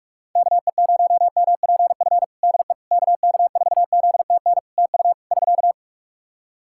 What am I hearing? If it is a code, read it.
Morse code, heard as a keyed tone, about 44 wpm, 700 Hz. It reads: KE0MPF DE KK4ZTN TU <SK>
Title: Practice QSO - Exchange 4 - Ending - 45wpm